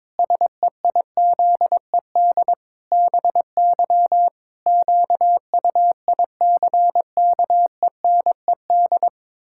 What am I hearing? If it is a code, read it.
Morse code, heard as a keyed tone, about 22 wpm, 700 Hz. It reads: SEIZED BY QUICKENED
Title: Pangram Challenge v2 - 22wpm